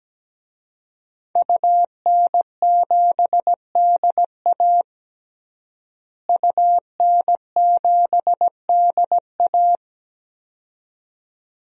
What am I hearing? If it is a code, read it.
Morse code, heard as a keyed tone, about 17 wpm, 700 Hz. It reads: UN7DA UN7DA
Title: DXCC-100 Call Signs - 5 Characters - Repeated 2x - 17wpm